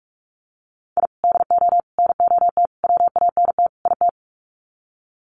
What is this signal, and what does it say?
Morse code, heard as a keyed tone, about 44 wpm, 700 Hz. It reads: I DO NOT WANT IT
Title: Sets of 5 Words Encore2 45wpm